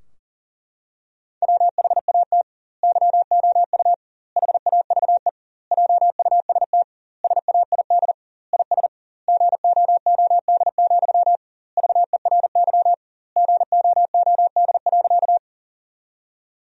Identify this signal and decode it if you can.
Morse code, heard as a keyed tone, about 40 wpm, 700 Hz. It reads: WHAT YOU HAVE JUST SAID IS GOOD, VERY GOOD.
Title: 2x Word-Spacing - Sentences from Top 200 Words - 40wpm